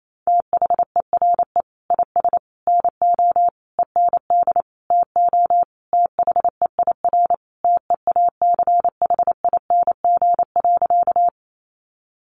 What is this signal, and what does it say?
Morse code, heard as a keyed tone, about 28 wpm, 700 Hz. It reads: THERE IS NO END TO THEIR TEACHING.
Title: Sentences from Top 1100 Words 28wpm